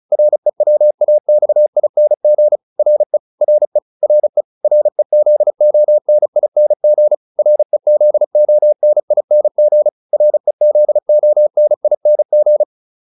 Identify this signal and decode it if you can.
Morse code, heard as a keyed tone, about 34 wpm, 600 Hz. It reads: REWAXING RE RE RE REZONING REZONING REZONING